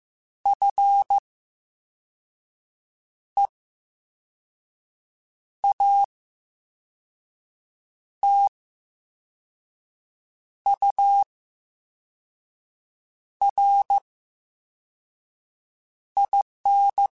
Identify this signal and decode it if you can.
Morse code, heard as a keyed tone, about 15 wpm, 800 Hz. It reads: F E A T U R I N
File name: Special INGs-mp3s / Special INGs-mp3s / VerySlowToFaster Ing @13